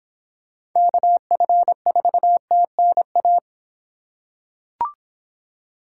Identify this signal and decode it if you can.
Morse code, heard as a keyed tone, about 26 wpm, 700 Hz. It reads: KF4TNA